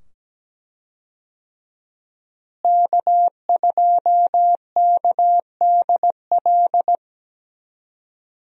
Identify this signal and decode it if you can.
Morse code, heard as a keyed tone, about 17 wpm, 700 Hz. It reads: K2KDL
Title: USA Call Signs - 5 Characters - Repeated 2x - 17wpm